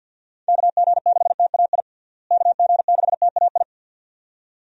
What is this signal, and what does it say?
Morse code, heard as a keyed tone, about 48 wpm, 700 Hz. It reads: KG6TAI KG6TAI
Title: USA Call Signs - 6 Characters - Repeated 2x - 50wpm